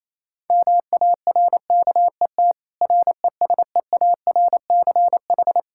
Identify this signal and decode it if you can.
Morse code, heard as a keyed tone, about 28 wpm, 700 Hz. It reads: MARKET RESEARCH
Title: Practice QSO - Exchange 3 - More Info - 28wpm